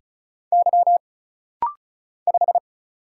Morse code, heard as a keyed tone, about 36 wpm, 700 Hz. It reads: Y 5
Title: Single Letter-Number - Rapid-Fire 35wpm